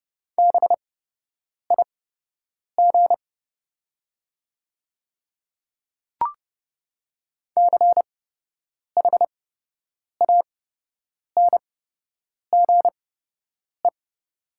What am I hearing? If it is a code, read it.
Morse code, heard as a keyed tone, about 30 wpm, spaced as for 8 wpm, 700 Hz. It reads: BIG CHANGE
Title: 8x Character Spacing - Top 200 Words - 30wpm